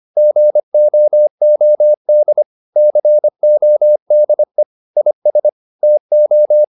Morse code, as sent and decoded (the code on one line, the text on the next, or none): --. --- --- -.. / -.-. --- -.. / .. ... / - ---
GOOD CODE IS TO